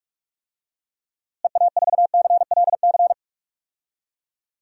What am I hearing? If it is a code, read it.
Morse code, heard as a keyed tone, about 46 wpm, 700 Hz. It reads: EA4CLC
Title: DXCC-100 Call Signs - 6 Characters - Repeated 2x - 45wpm